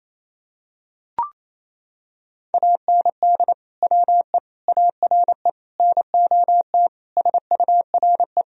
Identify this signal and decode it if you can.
Morse code, heard as a keyed tone, about 28 wpm, 700 Hz. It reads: AND WE ARE NOT SURE
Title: Sentences from Top 1000 Words - Review Part 1 - 28wpm